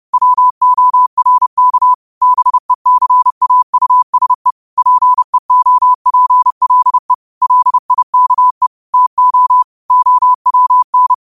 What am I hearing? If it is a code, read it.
Morse code, heard as a keyed tone, about 30 wpm, 1000 Hz. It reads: WORK BECAUSE PEOPLE LIKE TO OWN